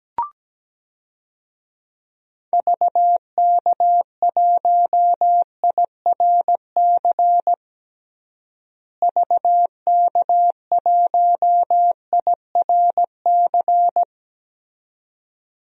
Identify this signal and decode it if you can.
Morse code, heard as a keyed tone, about 17 wpm, 700 Hz. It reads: VK1IRC VK1IRC
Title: DXCC-100 Call Signs - 6 Characters - Repeated 2x - 17wpm